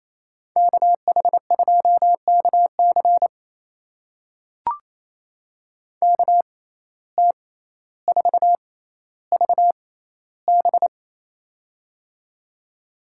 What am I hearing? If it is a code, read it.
Morse code, heard as a keyed tone, about 28 wpm, spaced as for 14 wpm, 700 Hz. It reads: KH2KC KT4VB
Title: USA Call Signs - 6x Character Spacing - 28wpm